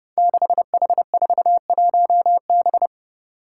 Morse code, heard as a keyed tone, about 30 wpm, 700 Hz. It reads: 6H41B
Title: Five Challenging Letter-Number Combinations 30wpm